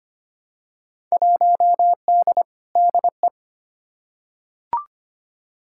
Morse code, heard as a keyed tone, about 25 wpm, 700 Hz. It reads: 1D DE
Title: Field Day Exchanges - 25wpm